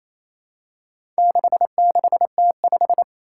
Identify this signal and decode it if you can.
Morse code, heard as a keyed tone, about 28 wpm, 700 Hz. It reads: 66T5
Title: Four Challenging Letter-Number Combinations 28wpm